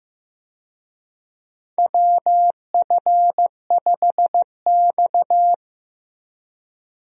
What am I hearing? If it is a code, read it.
Morse code, heard as a keyed tone, about 15 wpm, 700 Hz. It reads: WF5X
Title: USA Call Signs - 4 Characters - Repeated 3x - 15wpm